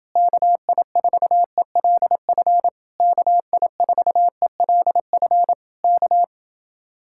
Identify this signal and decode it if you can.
Morse code, heard as a keyed tone, about 30 wpm, 700 Hz. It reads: KI4ELF KI4ELF K